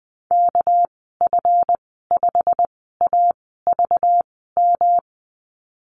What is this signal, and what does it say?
Morse code, heard as a keyed tone, about 20 wpm, spaced as for 15 wpm, 700 Hz. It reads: KF5AVM
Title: 207 Lesson 33g - Letter j - Call Signs 20wpm